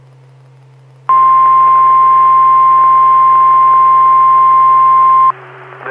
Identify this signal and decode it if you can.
Emergency Alert System (SAME) tones (the 1050 Hz attention tone). It is heard from 1.1 to 5.4 s.
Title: Saline, AK SKYWARN Repeater System - W5RHS